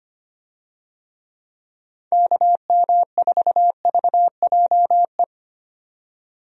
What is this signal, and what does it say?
Morse code, heard as a keyed tone, about 25 wpm, 700 Hz. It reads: KM4VJE
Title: USA Call Signs - 6 Characters - 25wpm